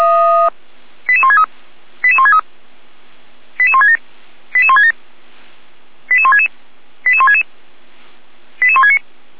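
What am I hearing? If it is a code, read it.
Five-tone selective call (ZVEI), heard as quick runs of five tones, tones of about 70 ms. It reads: 88163 88163 88167 88167 88166 88166 88169
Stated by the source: ZVEI five-tone selective call 88459 received by SDR.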